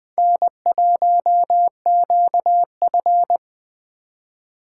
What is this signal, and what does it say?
Morse code, heard as a keyed tone, about 21 wpm, 700 Hz. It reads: N1QF